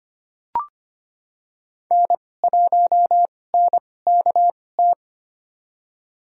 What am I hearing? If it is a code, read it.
Morse code, heard as a keyed tone, about 25 wpm, spaced as for 18 wpm, 700 Hz. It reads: N1NKT
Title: USA Call Signs - 5 Characters - 2x Character Spacing - 25wpm